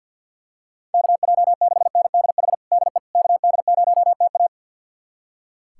Morse code, heard as a keyed tone, about 45 wpm, 700 Hz. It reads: KJ6NDH DE KD0TA
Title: USA Callsign DE Callsign 50wpm